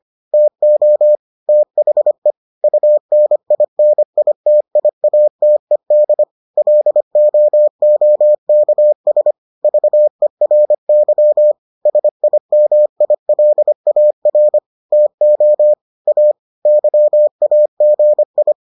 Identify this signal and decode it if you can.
Morse code, heard as a keyed tone, about 25 wpm, 600 Hz. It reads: TO THE UNINITIATED LOOKS VERY SIMILAR TO A YAGI